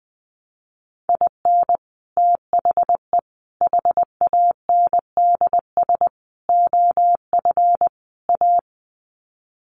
Morse code, heard as a keyed tone, about 20 wpm, 700 Hz. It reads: IN THE HANDS OF A
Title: Sets of 5 Words Encore 20wpm